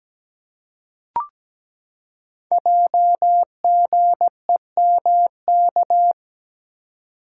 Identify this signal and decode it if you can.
Morse code, heard as a keyed tone, about 17 wpm, 700 Hz. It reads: JGEMK
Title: Code Groups - 5 Letters - 17wpm